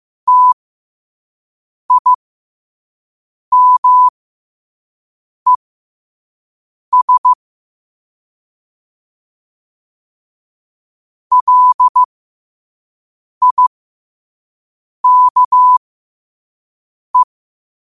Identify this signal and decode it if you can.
Morse code, heard as a keyed tone, about 15 wpm, spaced as for 5 wpm, 1000 Hz. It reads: TIMES LIKE
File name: 5wpm